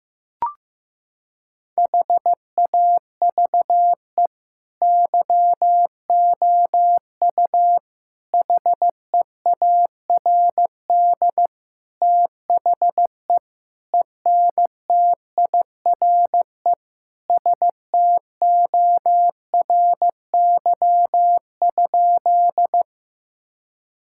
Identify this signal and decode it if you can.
Morse code, heard as a keyed tone, about 15 wpm, 700 Hz. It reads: HAVE YOU HEARD THE ENTIRE STORY?
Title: Sentences from Top 1000 Words - Review Part 5 - 15wpm